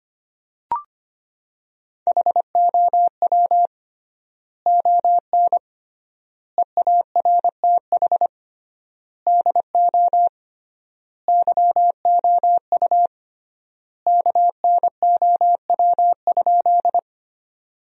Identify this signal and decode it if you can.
Morse code, heard as a keyed tone, about 25 wpm, 700 Hz. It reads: HOW ON EARTH DO YOU KNOW?
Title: 3x Word-Spacing - Sentences from Top 200 Words - 25wpm